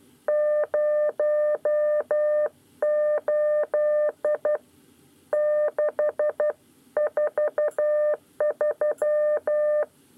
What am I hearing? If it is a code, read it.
Morse code, heard as a keyed tone, about 12 wpm, spaced as for 10 wpm, 600 Hz. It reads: 08643
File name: 9 lezione cw 2